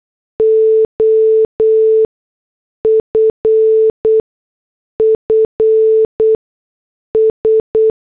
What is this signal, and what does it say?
Morse code, heard as a keyed tone, about 8 wpm, spaced as for 6 wpm, 440 Hz. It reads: OFFS
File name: morse / headlines